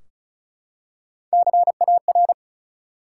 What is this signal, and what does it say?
Morse code, heard as a keyed tone, about 34 wpm, 700 Hz. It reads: CAR
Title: Three Letter Words - 35wpm